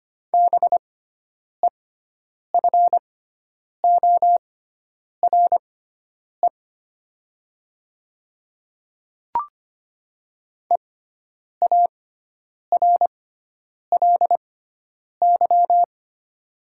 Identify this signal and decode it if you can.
Morse code, heard as a keyed tone, about 25 wpm, spaced as for 9 wpm, 690 Hz. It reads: BEFORE EARLY
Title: 6x Character Spacing - Top 500 Words - 25wpm